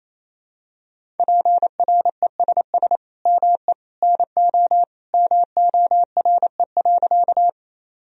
Morse code, heard as a keyed tone, about 27 wpm, 700 Hz. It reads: PRESS ME NO MORE.
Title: Sentences from Top 1000 Words - Review Part 1 - 28wpm